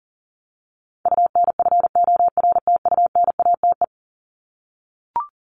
Morse code, heard as a keyed tone, about 40 wpm, 700 Hz. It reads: UNFORTUNATE
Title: LY Suffix - Top 100 Words with Base Words 40wpm